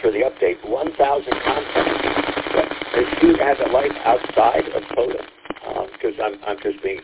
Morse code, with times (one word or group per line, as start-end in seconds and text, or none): none